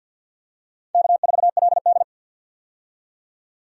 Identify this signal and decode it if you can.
Morse code, heard as a keyed tone, about 48 wpm, 700 Hz. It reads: K4LD
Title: USA Call Signs - 4 Characters - 50wpm